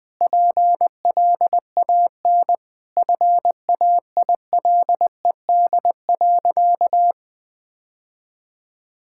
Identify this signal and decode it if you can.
Morse code, heard as a keyed tone, about 20 wpm, 700 Hz. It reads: PLAN FAILED.
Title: Sentences with 9-Word Phrases - 20wpm